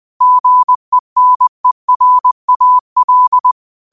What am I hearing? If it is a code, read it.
Morse code, heard as a keyed tone, about 20 wpm, 1000 Hz. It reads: GENERAL
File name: qotdfiles / 20wpm